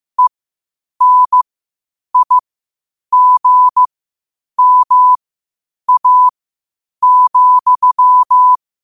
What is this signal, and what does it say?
Morse code, heard as a keyed tone, about 15 wpm, spaced as for 8 wpm, 1000 Hz. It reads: ENIGMA,
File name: qotdfiles / 8wpm